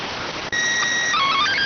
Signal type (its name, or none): SSTV (slow-scan television)